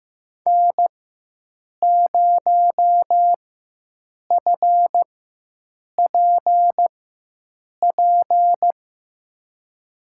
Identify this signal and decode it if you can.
Morse code, heard as a keyed tone, about 15 wpm, spaced as for 7 wpm, 700 Hz. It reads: N0FPP
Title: USA Call Signs - 5 Characters - 4x Character Spacing - 15wpm